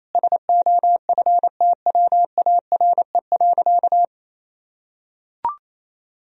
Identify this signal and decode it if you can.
Morse code, heard as a keyed tone, about 28 wpm, 700 Hz. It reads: SOFTWARE.
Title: Sentences from Top 1000 Words - Review Part 3 - 28wpm